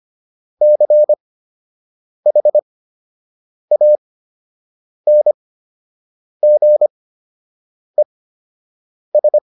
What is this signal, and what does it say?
Morse code, heard as a keyed tone, about 25 wpm, spaced as for 7 wpm, 600 Hz. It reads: CHANGES